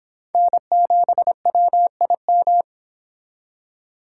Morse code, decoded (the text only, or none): N7WIM